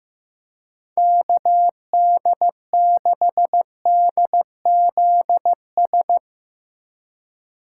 Morse code, heard as a keyed tone, about 16 wpm, 700 Hz. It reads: KD6DZS
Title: USA Call Signs - 6 Characters - 3x Character Spacing - 15wpm